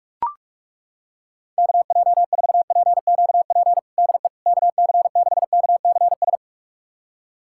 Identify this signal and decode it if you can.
Morse code, heard as a keyed tone, about 41 wpm, 700 Hz. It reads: KJ4PQP DE KC6KCS
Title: USA Callsign DE Callsign - 6 Characters - 45wpm